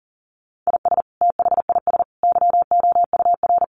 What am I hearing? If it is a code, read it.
Morse code, heard as a keyed tone, about 40 wpm, 700 Hz. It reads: IS THIS YOUR
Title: Sentences from Top 1500 Words 40wpm